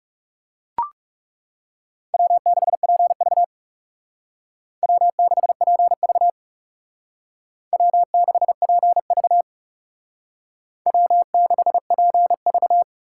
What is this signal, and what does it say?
Morse code, heard as a keyed tone, about 36 wpm, 700 Hz. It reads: W6PV W6PV W6PV W6PV
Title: Speed Racing - USA 1x2 Call Signs 30wpm